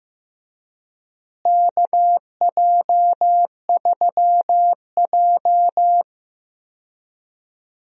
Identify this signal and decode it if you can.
Morse code, heard as a keyed tone, about 15 wpm, 700 Hz. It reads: KJ3J
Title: USA Call Signs - 4 Characters - Repeated 3x - 15wpm